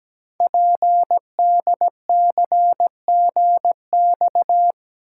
Morse code, heard as a keyed tone, about 17 wpm, 700 Hz. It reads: PDCGX
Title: Code Groups - 5 Letters - 17wpm